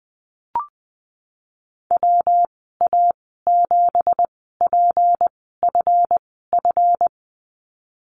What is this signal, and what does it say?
Morse code, heard as a keyed tone, about 20 wpm, 700 Hz. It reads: W A 7 P F F
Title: 238 Lesson 38g - Letter q - Call Signs 20wpm